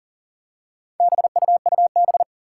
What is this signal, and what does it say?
Morse code, heard as a keyed tone, about 38 wpm, 700 Hz. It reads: BUUB
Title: Four Challenging Letter-Number Combinations 40wpm